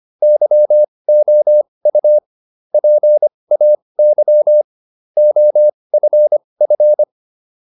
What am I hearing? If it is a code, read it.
Morse code, heard as a keyed tone, about 25 wpm, spaced as for 20 wpm, 600 Hz. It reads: YOU PAY OFF